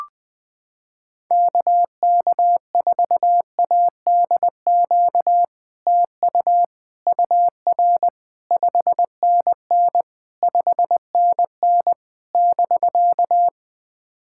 Morse code, heard as a keyed tone, about 20 wpm, 700 Hz. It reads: KK4ADQ TU UR 5NN 5NN <BK>
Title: POTA QSO Practice - 20wpm